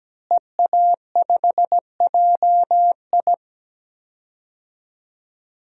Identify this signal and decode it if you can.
Morse code, heard as a keyed tone, about 17 wpm, 700 Hz. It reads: EA5JI
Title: DXCC-100 Call Signs - 5 Characters - Repeated 3x - 17wpm